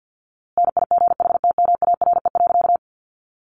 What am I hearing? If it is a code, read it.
Morse code, heard as a keyed tone, about 50 wpm, 700 Hz. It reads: NIGHTMARE.
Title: A Sound Of Thunder 50wpm